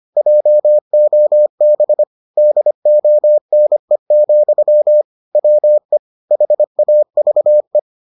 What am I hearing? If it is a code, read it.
Morse code, heard as a keyed tone, about 25 wpm, 600 Hz. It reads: JOB DONE, WE HAVE